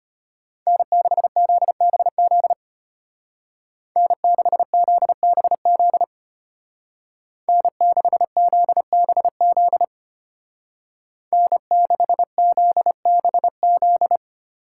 Morse code, heard as a keyed tone, about 30 wpm, 700 Hz. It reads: N6ZBZ N6ZBZ N6ZBZ N6ZBZ
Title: Speed Racing - USA 1x3 Call Signs 25wpm